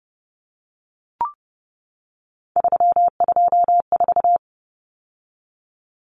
Morse code, heard as a keyed tone, about 30 wpm, 700 Hz. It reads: 324
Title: Common Cut Numbers - 3-Digit 30wpm